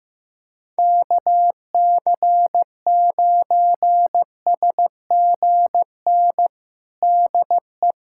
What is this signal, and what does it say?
Morse code, heard as a keyed tone, about 15 wpm, 700 Hz. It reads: KC9SGN DE
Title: USA Callsign DE Callsign 15wpm